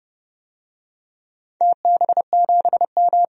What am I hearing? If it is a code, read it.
Morse code, heard as a keyed tone, about 30 wpm, 700 Hz. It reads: TB7M
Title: DXCC-100 Call Signs - 4 Characters - 30wpm